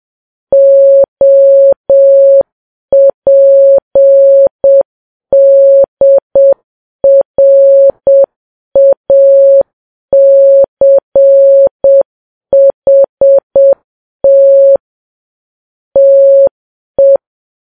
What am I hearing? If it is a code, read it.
Morse code, heard as a keyed tone, about 7 wpm, 550 Hz. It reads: OPDRACHT TE
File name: Files Web tls / Morse on9fv / morse7kps